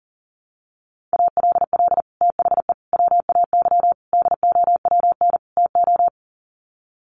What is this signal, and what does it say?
Morse code, heard as a keyed tone, about 40 wpm, 700 Hz. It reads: ALL THE WAY DOWN TO
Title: Sets of 5 Words Encore2 40wpm